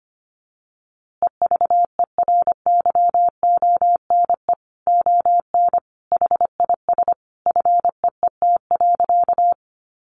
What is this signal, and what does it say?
Morse code, heard as a keyed tone, about 25 wpm, 700 Hz. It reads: EVERYONE ON HIS FEET.
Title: A Sound Of Thunder 25wpm